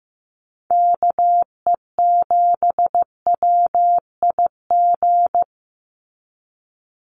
Morse code, heard as a keyed tone, about 15 wpm, 700 Hz. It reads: KE7WIG
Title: USA Call Signs 15wpm